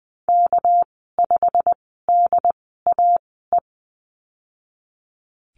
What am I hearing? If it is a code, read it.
Morse code, heard as a keyed tone, about 20 wpm, spaced as for 15 wpm, 700 Hz. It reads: K5DAE
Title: 219 Lesson 35g - Number 0 - Call Signs 20wpm